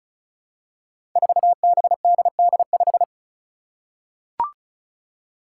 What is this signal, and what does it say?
Morse code, heard as a keyed tone, about 34 wpm, 700 Hz. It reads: 4BDD5
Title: Five Challenging Letter-Number Combinations 35wpm